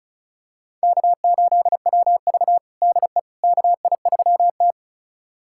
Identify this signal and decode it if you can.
Morse code, heard as a keyed tone, about 36 wpm, 700 Hz. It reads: K8WV DE KI3T
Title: USA Callsign DE Callsign - 4 Characters - 35wpm